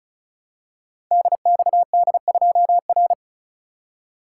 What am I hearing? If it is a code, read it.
Morse code, heard as a keyed tone, about 36 wpm, 700 Hz. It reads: DXD2R
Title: Code Groups - 5 Letter:Numbers - 35wpm